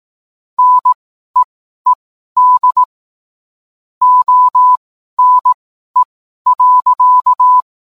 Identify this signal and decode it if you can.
Morse code, heard as a keyed tone, about 18 wpm, spaced as for 12 wpm, 1000 Hz. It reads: NEED ONE.